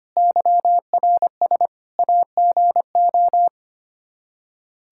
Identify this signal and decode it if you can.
Morse code, heard as a keyed tone, about 25 wpm, 700 Hz. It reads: YRS AGO
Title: Practice QSO - Exchange 3 - More Info - 25wpm